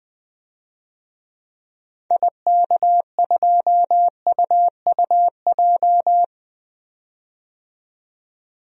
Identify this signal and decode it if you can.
Morse code, heard as a keyed tone, about 20 wpm, 700 Hz. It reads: IK2UUJ
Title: DXCC-100 Call Signs - 6 Characters - 20wpm